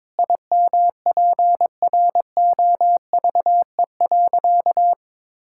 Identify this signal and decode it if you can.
Morse code, heard as a keyed tone, about 22 wpm, 700 Hz. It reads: IMPROVE.
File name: practice / 22wpm / Sentences with 4-Word Phrases - 22wpm